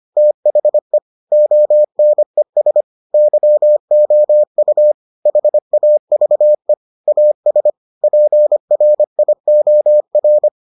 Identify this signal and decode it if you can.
Morse code, heard as a keyed tone, about 25 wpm, 600 Hz. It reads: THE ONES YOU HAVE AS PRIOR